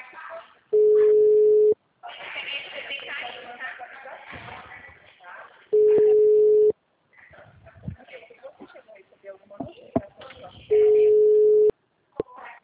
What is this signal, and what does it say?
Ringback tone (Europe, 425 Hz), with speech.
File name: out- -2025- - - .51155